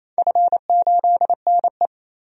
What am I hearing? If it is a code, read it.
Morse code, heard as a keyed tone, about 28 wpm, 700 Hz. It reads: F8NE
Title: DXCC-100 Call Signs - 4 Characters - 28wpm